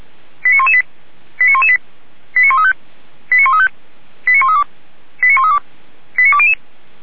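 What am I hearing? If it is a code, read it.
Five-tone selective call (ZVEI), heard as quick runs of five tones, tones of about 70 ms. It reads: 89118 89118 89135 89135 89132 89132 89200
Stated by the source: ZVEI five-tone call 89103 received on SDR.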